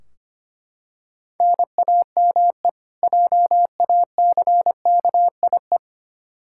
Morse code, heard as a keyed tone, about 25 wpm, 700 Hz. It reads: NAME JACKIE
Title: Practice QSO - Exchange 1 - Beginning - 25wpm